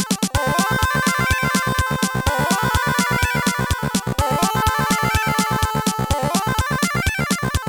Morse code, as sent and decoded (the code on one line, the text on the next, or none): none